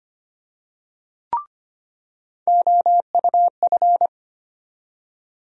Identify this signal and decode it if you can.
Morse code, heard as a keyed tone, about 25 wpm, 700 Hz. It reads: OUF
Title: USA Call Sign Postfixes - 3 Letters 25wpm